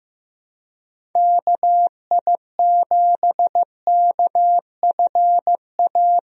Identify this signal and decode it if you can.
Morse code, heard as a keyed tone, about 16 wpm, 700 Hz. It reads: KI7KFA
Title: USA Call Signs - 6 Characters - Repeated 3x - 15wpm